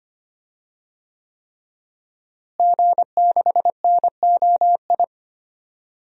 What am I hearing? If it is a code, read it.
Morse code, heard as a keyed tone, about 25 wpm, 700 Hz. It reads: G6NOI
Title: DXCC-100 Call Signs - 5 Characters - Repeated 3x - 25wpm